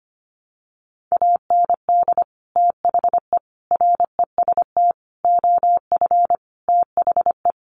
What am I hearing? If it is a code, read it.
Morse code, heard as a keyed tone, about 25 wpm, 700 Hz. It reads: AND THE REST OF THE
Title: Sets of 5 Words 25wpm